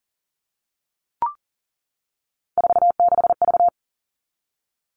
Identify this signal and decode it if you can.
Morse code, heard as a keyed tone, about 40 wpm, 700 Hz. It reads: 46V
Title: All Cut Numbers - 3-Digit 40wpm